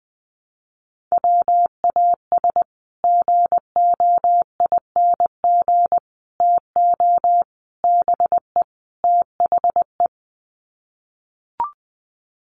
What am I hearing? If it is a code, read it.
Morse code, heard as a keyed tone, about 20 wpm, 700 Hz. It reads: WAS GOING TO BE THE
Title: Sets of 5 Words Encore2 20wpm